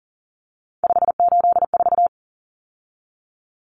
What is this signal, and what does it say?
Morse code, heard as a keyed tone, about 37 wpm, 700 Hz. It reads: 584
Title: Common Cut Numbers - 3-Digit 40wpm